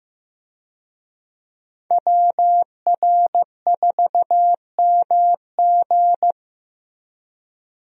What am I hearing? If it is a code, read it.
Morse code, heard as a keyed tone, about 15 wpm, 700 Hz. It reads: WR4MG
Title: USA Call Signs - 5 Characters - Repeated 3x - 15wpm